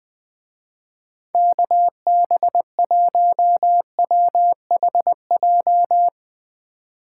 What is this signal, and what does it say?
Morse code, heard as a keyed tone, about 22 wpm, 700 Hz. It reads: KB1WHJ
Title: Speed Racing - USA 2x3 Call Signs 20wpm